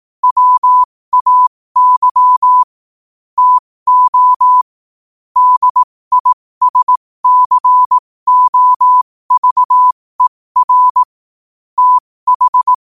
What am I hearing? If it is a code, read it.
Morse code, heard as a keyed tone, about 18 wpm, spaced as for 15 wpm, 1000 Hz. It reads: WAY TO DISCOVER TH
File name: qotdfiles / 15wpm